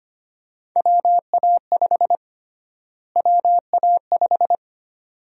Morse code, heard as a keyed tone, about 25 wpm, 700 Hz. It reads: WA5 WA5
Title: USA Call Sign Prefixes - Repeated 2x - 25wpm